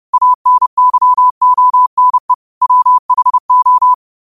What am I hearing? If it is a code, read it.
Morse code, heard as a keyed tone, about 30 wpm, 1000 Hz. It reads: ANYONE WHO